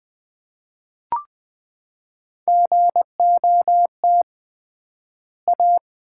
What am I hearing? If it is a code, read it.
Morse code, heard as a keyed tone, about 20 wpm, 700 Hz. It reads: GOT A
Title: 3x Word-Spacing - Sets of 4 Words - 20wpm